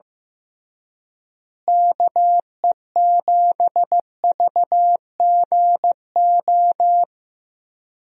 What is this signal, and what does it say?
Morse code, heard as a keyed tone, about 15 wpm, 700 Hz. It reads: KE7VGO
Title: USA Call Signs - 6 Characters - Repeated 3x - 15wpm